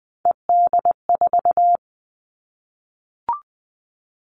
Morse code, heard as a keyed tone, about 20 wpm, 700 Hz. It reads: ED4
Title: All Cut Numbers - 3-Digit 20wpm